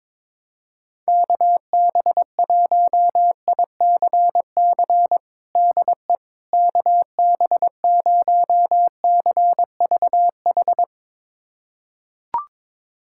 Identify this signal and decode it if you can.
Morse code, heard as a keyed tone, about 22 wpm, 700 Hz. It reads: KB1ICC DE KB0CVH
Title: USA Callsign DE Callsign 22wpm